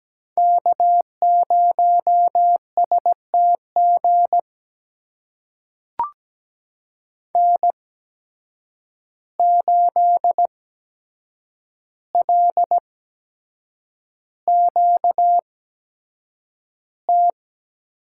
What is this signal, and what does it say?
Morse code, heard as a keyed tone, about 17 wpm, spaced as for 7 wpm, 700 Hz. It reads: K0STG N8LQT
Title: USA Call Signs - 8x Character Spacing - 17wpm